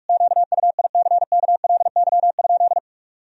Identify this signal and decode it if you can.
Morse code, heard as a keyed tone, about 42 wpm, 700 Hz. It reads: QUICKLY?
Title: Sentences from Top 1000 Words - Review Part 3 - 45wpm